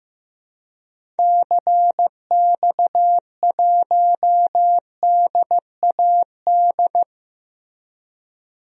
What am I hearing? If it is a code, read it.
Morse code, heard as a keyed tone, about 15 wpm, 700 Hz. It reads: CX1DAD
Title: DXCC-100 Call Signs - 6 Characters - Repeated 2x - 15wpm